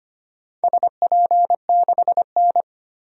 Morse code, heard as a keyed tone, about 25 wpm, 700 Hz. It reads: SP6N
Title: DXCC-100 Call Signs - 4 Characters - 25wpm